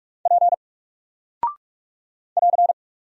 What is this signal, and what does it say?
Morse code, heard as a keyed tone, about 44 wpm, 700 Hz. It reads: P <AR>
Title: Single Character (Letter:Number:Punctuation:Prosign) 45wpm